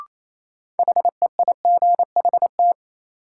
Morse code, heard as a keyed tone, about 28 wpm, 700 Hz. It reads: HEIGHT
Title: Sentences from Top 1600 Words - New Word Drill Alt - 28wpm